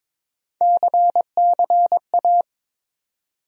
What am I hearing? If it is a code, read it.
Morse code, heard as a keyed tone, about 21 wpm, 700 Hz. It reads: CCA
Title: USA Call Sign Postfixes - 3 Letters 22wpm